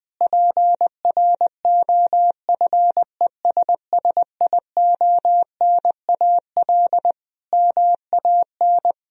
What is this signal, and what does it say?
Morse code, heard as a keyed tone, about 20 wpm, 700 Hz. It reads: PROFESSIONAL MAN
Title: Sentences from Top 1000 Words - Review Part 4 - 20wpm